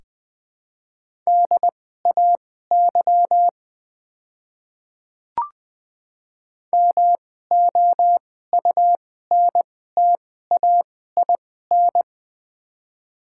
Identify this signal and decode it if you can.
Morse code, heard as a keyed tone, about 21 wpm, spaced as for 15 wpm, 700 Hz. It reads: DAY MOUNTAIN
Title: 2x Character Spacing - Top 300 Words - 20wpm